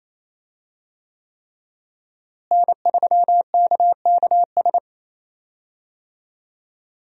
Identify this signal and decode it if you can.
Morse code, heard as a keyed tone, about 28 wpm, 700 Hz. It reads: N3KKS
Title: USA Call Signs - 5 Characters - Repeated 3x - 28wpm